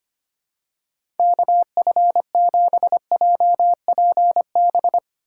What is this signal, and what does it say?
Morse code, heard as a keyed tone, about 25 wpm, 700 Hz. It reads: KF7JPB
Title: USA Call Signs - 6 Characters - 4x Character Spacing - 25wpm